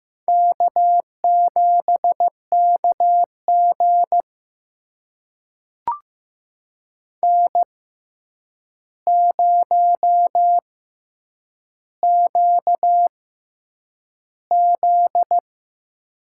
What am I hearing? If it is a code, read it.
Morse code, heard as a keyed tone, about 15 wpm, spaced as for 8 wpm, 700 Hz. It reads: K7KG N0QZ
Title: USA Call Signs - 4 Characters - 6x Character Spacing - 15wpm